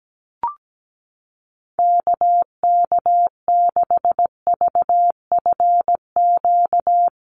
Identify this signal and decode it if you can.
Morse code, heard as a keyed tone, about 17 wpm, 700 Hz. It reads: KK6VFQ
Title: USA Call Signs 17wpm